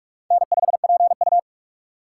Morse code, heard as a keyed tone, about 47 wpm, 700 Hz. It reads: N5PU
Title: K1USN SST QSO Practice - 45wpm